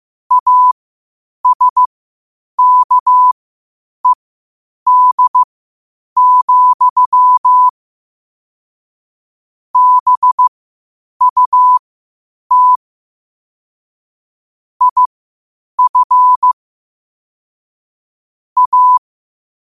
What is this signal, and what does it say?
Morse code, heard as a keyed tone, about 15 wpm, spaced as for 8 wpm, 1000 Hz. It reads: ASKED, BUT IF A
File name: qotdfiles / 8wpm